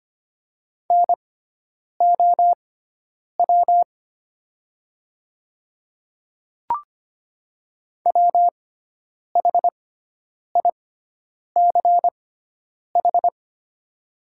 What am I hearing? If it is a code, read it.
Morse code, heard as a keyed tone, about 25 wpm, spaced as for 9 wpm, 700 Hz. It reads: NOW WHICH
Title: 6x Character Spacing - Top 100 Words - 25wpm